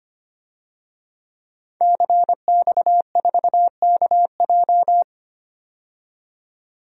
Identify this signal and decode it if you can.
Morse code, heard as a keyed tone, about 25 wpm, 700 Hz. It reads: CX4KJ
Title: DXCC-100 Call Signs - 5 Characters - 25wpm